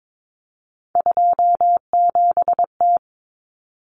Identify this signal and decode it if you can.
Morse code, heard as a keyed tone, about 22 wpm, 700 Hz. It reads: 27T
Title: Common Cut Numbers - 3-Digit 22wpm